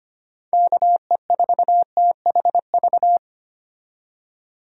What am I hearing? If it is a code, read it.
Morse code, heard as a keyed tone, about 25 wpm, 700 Hz. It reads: KE4THV
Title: USA Call Signs - 6 Characters - 25wpm